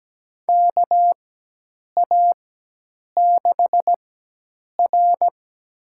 Morse code, heard as a keyed tone, about 17 wpm, spaced as for 8 wpm, 700 Hz. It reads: KA6R
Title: USA Call Signs - 4 Characters - 4x Character Spacing - 17wpm